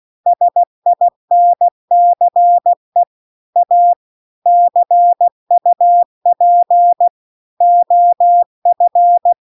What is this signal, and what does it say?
Morse code, heard as a keyed tone, about 16 wpm, 700 Hz. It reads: SINCE A CUP OF